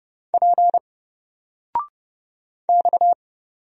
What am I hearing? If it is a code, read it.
Morse code, heard as a keyed tone, about 31 wpm, 700 Hz. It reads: P X
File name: Single-Letter-Number-30wpm